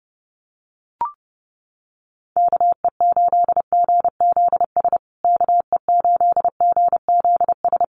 Morse code, heard as a keyed tone, about 30 wpm, 700 Hz. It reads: KE8GZS KE8GZS
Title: USA Call Signs Repeated 3x 30wpm